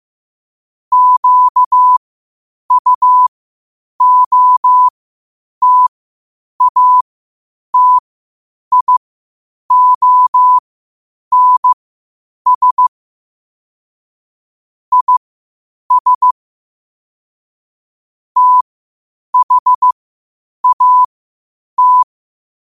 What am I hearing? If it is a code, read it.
Morse code, heard as a keyed tone, about 15 wpm, spaced as for 8 wpm, 1000 Hz. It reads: QUOTATIONS IS THAT
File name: qotdfiles / 8wpm